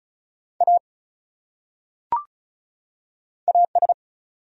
Morse code, heard as a keyed tone, about 36 wpm, 700 Hz. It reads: A AS
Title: Top 100 Words 35wpm